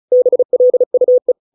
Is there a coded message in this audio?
Yes: Morse code from 0.1 to 1.3 s, heard as a keyed tone, about 34 wpm, 500 Hz. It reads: BLUE